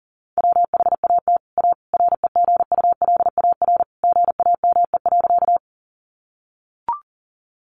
Morse code, heard as a keyed tone, about 40 wpm, 700 Hz. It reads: WHAT A REGULAR GAME.
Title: Sentences from Top 900 Words 40wpm